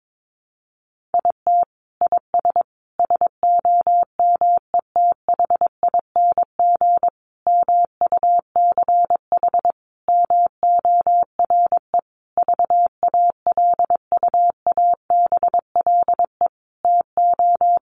Morse code, heard as a keyed tone, about 22 wpm, 700 Hz. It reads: IT IS SOMETHING MUCH MORE VALUABLE TO
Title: Sentences from Top 1100 Words 22wpm